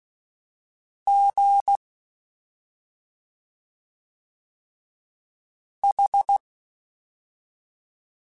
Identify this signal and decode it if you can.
Morse code, heard as a keyed tone, about 16 wpm, spaced as for 2 wpm, 800 Hz. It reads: GH
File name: Special INGs-mp3s / Special INGs-mp3s / Slow ing @10